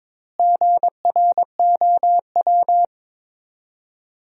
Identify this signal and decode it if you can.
Morse code, heard as a keyed tone, about 22 wpm, 700 Hz. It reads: GROW
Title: 4x Word-Spacing - Sentences from Top 300 Words - 22wpm